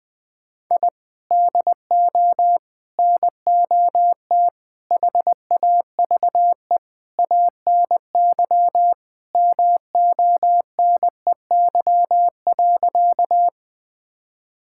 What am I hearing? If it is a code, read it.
Morse code, heard as a keyed tone, about 21 wpm, 700 Hz. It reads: I DO NOT HAVE ANY MONEY.
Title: Sentences with 5-Word Phrases - 20wpm